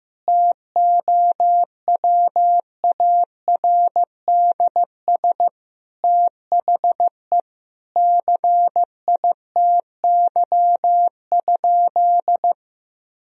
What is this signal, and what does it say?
Morse code, heard as a keyed tone, about 16 wpm, 700 Hz. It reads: TOWARDS THE CITY?
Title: Sentences from Top 1000 Words - Review Part 2 - 15wpm